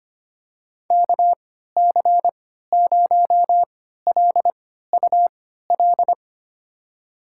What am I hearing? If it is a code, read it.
Morse code, heard as a keyed tone, about 25 wpm, spaced as for 14 wpm, 700 Hz. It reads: KC0LUL
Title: USA Call Signs - 6 Characters - 3x Character Spacing - 25wpm